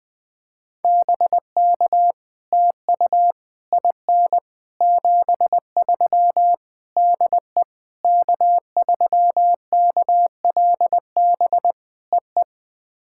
Morse code, heard as a keyed tone, about 20 wpm, 700 Hz. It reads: BK TU IN 73 DE K3KLB EE
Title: POTA QSO Practice - 20wpm